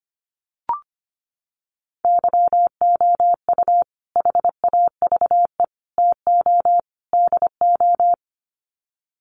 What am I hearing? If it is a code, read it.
Morse code, heard as a keyed tone, about 25 wpm, 700 Hz. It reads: YOU HAVE TO DO
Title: Sets of 4 Words 25wpm